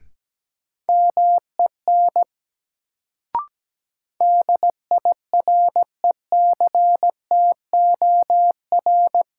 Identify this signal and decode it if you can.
Morse code, heard as a keyed tone, about 17 wpm, 700 Hz. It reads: MEN DIRECTOR
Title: Sets of 3 Words - New Word Drill - 17wpm